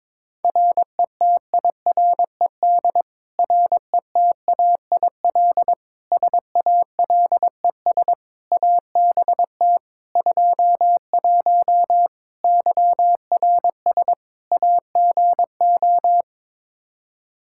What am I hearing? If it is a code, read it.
Morse code, heard as a keyed tone, about 22 wpm, 700 Hz. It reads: RETIRED RETAIL SALES ABT 21 YRS AGO
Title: Practice QSO - Exchange 3 - More Info - 22wpm